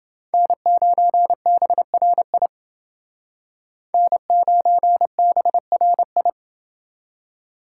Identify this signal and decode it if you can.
Morse code, heard as a keyed tone, about 28 wpm, 700 Hz. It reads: N9BRI N9BRI